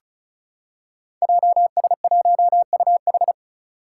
Morse code, heard as a keyed tone, about 36 wpm, 700 Hz. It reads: JS1UH
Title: Five Challenging Letter-Number Combinations 35wpm